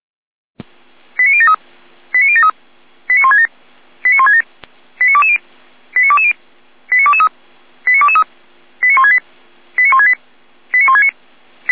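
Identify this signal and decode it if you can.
Five-tone selective call (ZVEI), heard as quick runs of five tones, tones of about 70 ms. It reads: 89073 89073 89167 89167 89220 89220 89223 89223 89168 89168 89169
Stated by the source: ZVEI five-tone call 89123 received via SDR.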